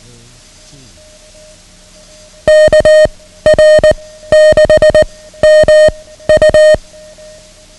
Morse code, heard as a keyed tone, about 19 wpm, 620 Hz. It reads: K R 6 M U
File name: ham radio / CodeQuick / Code Quick 09